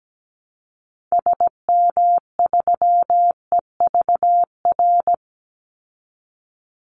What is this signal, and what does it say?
Morse code, heard as a keyed tone, about 17 wpm, 700 Hz. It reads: SM3EVR
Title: Call Signs Repeated 2x - 17wpm